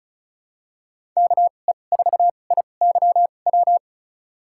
Morse code, heard as a keyed tone, about 34 wpm, spaced as for 25 wpm, 700 Hz. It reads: KE4IYW
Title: USA Call Signs - 2x Character Spacing - 35wpm